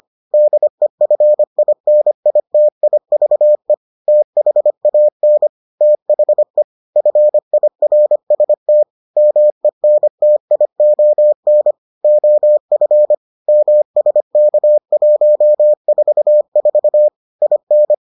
Morse code, heard as a keyed tone, about 25 wpm, 600 Hz. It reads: DEFINITIVE THAN THE FIRST MENTION OF MSK144 IN